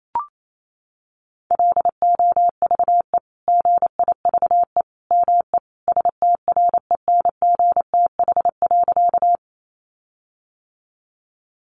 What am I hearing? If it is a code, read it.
Morse code, heard as a keyed tone, about 28 wpm, 700 Hz. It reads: LOVE GIVE ME STRENGTH.
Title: Sentences from Top 1300 Words 28wpm